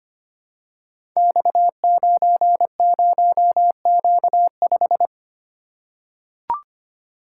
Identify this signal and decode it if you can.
Morse code, heard as a keyed tone, about 25 wpm, 700 Hz. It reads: X90Q5
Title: Code Groups - 5 Letter:Numbers - 25wpm